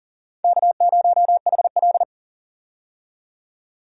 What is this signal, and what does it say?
Morse code, heard as a keyed tone, about 40 wpm, 690 Hz. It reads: K0HL
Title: USA Call Signs - 4 Characters - 40wpm